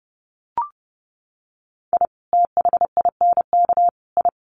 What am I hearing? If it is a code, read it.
Morse code, heard as a keyed tone, about 30 wpm, 700 Hz. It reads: I THINK I
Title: Sets of 3 Words 30wpm